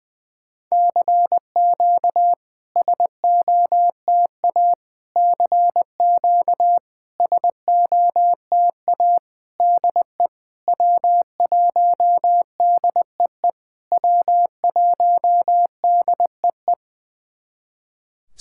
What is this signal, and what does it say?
Morse code, heard as a keyed tone, about 20 wpm, 700 Hz. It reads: CQ SOTA CQ SOTA DE W1DEE W1DEE
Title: SOTA QSO Practice 20wpm